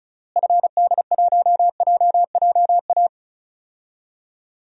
Morse code, heard as a keyed tone, about 36 wpm, 700 Hz. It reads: FD1JJA
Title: DXCC-100 Call Signs - 6 Characters - Repeated 3x - 35wpm